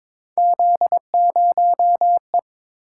Morse code, heard as a keyed tone, about 23 wpm, 700 Hz. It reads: Z0E
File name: practice / 22wpm / Three Letter-Number Combinations - 22wpm